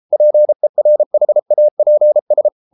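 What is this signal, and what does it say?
Morse code, heard as a keyed tone, about 33 wpm, 600 Hz. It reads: PERHAPS
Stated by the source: Morse code practice at 32 wpm from CW med Gnister, reading Ken Follett's novel Never in English.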